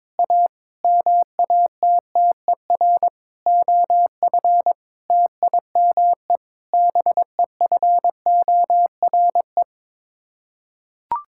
Morse code, heard as a keyed tone, about 22 wpm, 700 Hz. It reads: A MATTER OF TIME BEFORE
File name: practice / 22wpm / Sets of 8 Words - 22wpm